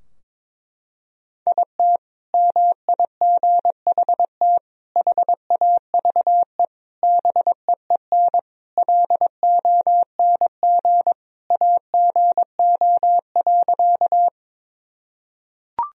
Morse code, heard as a keyed tone, about 22 wpm, 700 Hz. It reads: IT MIGHT HAVE BEEN LONG AGO.
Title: Sentences from Top 1000 Words - Review Part 3 - 22wpm